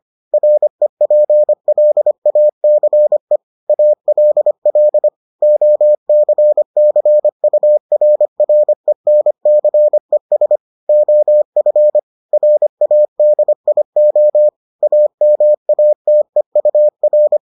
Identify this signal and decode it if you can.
Morse code, heard as a keyed tone, about 25 wpm, 600 Hz. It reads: REPLACE ALL OCCURRENCES OF RADIO AMATEUR